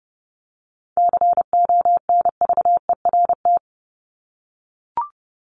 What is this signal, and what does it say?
Morse code, heard as a keyed tone, about 30 wpm, 700 Hz. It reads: CONVERT
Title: Top 1751-2000 Words 30wpm